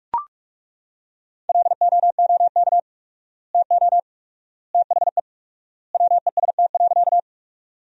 Morse code, heard as a keyed tone, about 42 wpm, 700 Hz. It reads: LOOK TO THE WEST.
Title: 4x Word-Spacing - Sentences from Top 400 Words - 45wpm